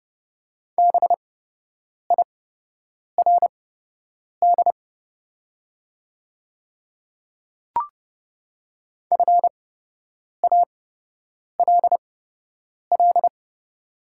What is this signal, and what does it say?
Morse code, heard as a keyed tone, about 30 wpm, spaced as for 8 wpm, 690 Hz. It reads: BIRD FALL
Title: 8x Character Spacing - Top 500 Words - 30wpm